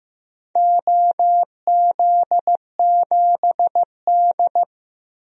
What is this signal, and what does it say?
Morse code, heard as a keyed tone, about 15 wpm, 700 Hz. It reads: OZ7D
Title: DXCC-100 Call Signs - 4 Characters - 15wpm